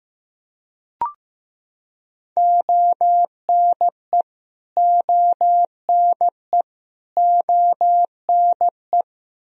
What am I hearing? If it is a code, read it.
Morse code, heard as a keyed tone, about 15 wpm, 700 Hz. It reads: ONE ONE ONE
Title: Top 100 Words Repeated 15wpm